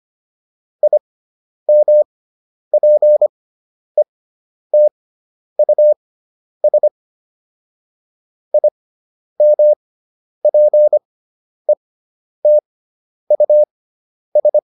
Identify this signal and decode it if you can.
Morse code, heard as a keyed tone, about 25 wpm, spaced as for 10 wpm, 600 Hz. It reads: IMPETUS IMPETUS